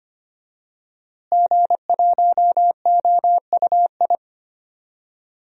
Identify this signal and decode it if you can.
Morse code, heard as a keyed tone, about 25 wpm, 700 Hz. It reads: G1OUI
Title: DXCC-100 Call Signs - 5 Characters - Repeated 2x - 25wpm